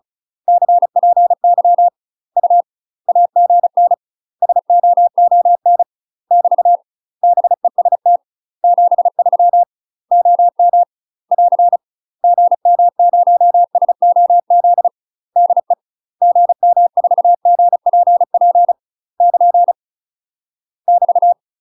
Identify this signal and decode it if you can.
Morse code, heard as a keyed tone, about 34 wpm, 700 Hz. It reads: CPY U AGN SOON <BT> BEST 73 OM <AR> GM0SOZ DE GM4GPP <KN> <BT>
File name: UK MASTER QSOs TRAINING/35WPM